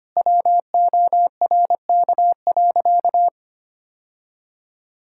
Morse code, heard as a keyed tone, about 25 wpm, 700 Hz. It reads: WORK.
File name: practice / 25wpm / Sentences with 3-Word Phrases - 25wpm